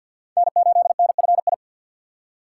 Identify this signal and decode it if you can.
Morse code, heard as a keyed tone, about 47 wpm, 700 Hz. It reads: N8NFI
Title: USA Call Signs - 5 Characters - 2x Character Spacing - 50wpm